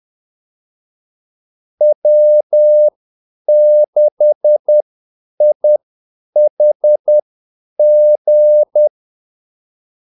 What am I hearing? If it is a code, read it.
Morse code, heard as a keyed tone, about 10 wpm, spaced as for 8 wpm, 600 Hz. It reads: W6IHG
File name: Callsigns-10-8wpm